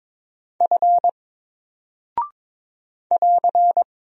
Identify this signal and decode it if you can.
Morse code, heard as a keyed tone, about 22 wpm, 700 Hz. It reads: F <AR>
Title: Single Character (Letter:Number:Punctuation:Prosign) 22wpm